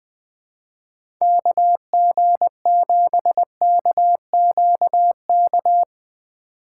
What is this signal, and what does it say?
Morse code, heard as a keyed tone, about 20 wpm, 700 Hz. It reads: KG7KQK
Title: USA Call Signs - 3x Character Spacing - 20wpm